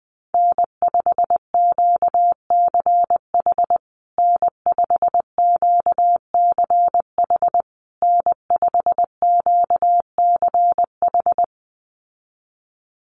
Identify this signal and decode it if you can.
Morse code, heard as a keyed tone, about 19 wpm, 700 Hz. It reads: N5QCH N5QCH N5QCH
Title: USA Call Signs Repeated 3x 20wpm